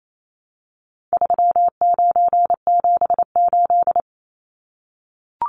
Morse code, heard as a keyed tone, about 28 wpm, 700 Hz. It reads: 3978
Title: Numbers - 4-Digit 28wpm